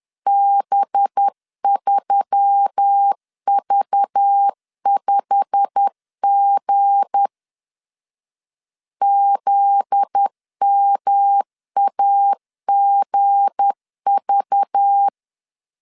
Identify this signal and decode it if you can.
Morse code, heard as a keyed tone, about 10 wpm, 800 Hz. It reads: B3V5G ZMAGV
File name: LEZIONI / 8 YJ45 / 8 YJ45 10wpm 2549